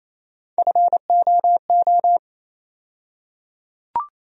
Morse code, heard as a keyed tone, about 28 wpm, 700 Hz. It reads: FOO
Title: USA Call Sign Postfixes - 3 Letters 28wpm